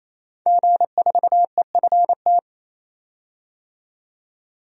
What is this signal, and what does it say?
Morse code, heard as a keyed tone, about 27 wpm, 700 Hz. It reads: G4EFT
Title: DXCC-100 Call Signs - 5 Characters - 28wpm